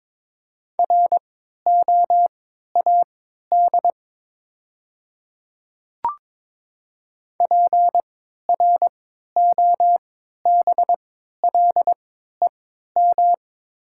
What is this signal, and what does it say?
Morse code, heard as a keyed tone, about 22 wpm, spaced as for 13 wpm, 700 Hz. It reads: ROAD PROBLEM
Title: 3x Character Spacing - Top 500 Words - 22wpm